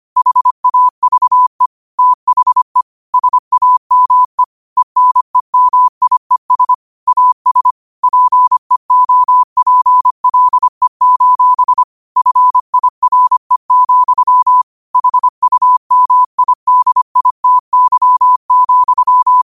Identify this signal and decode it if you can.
Morse code, heard as a keyed tone, about 25 wpm, 1000 Hz. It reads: HAVE THE SAME ENEMIES AS PEOPLE: FIRE, HUMIDITY,